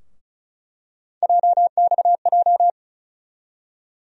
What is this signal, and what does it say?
Morse code, heard as a keyed tone, about 36 wpm, 700 Hz. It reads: JXJ
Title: USA Call Sign Postfixes - 3 Letters 35wpm